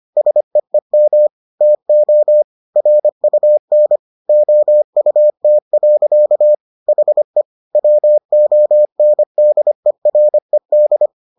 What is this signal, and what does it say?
Morse code, heard as a keyed tone, about 25 wpm, 600 Hz. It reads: SEEM TO RUN OUT. HE WONDERED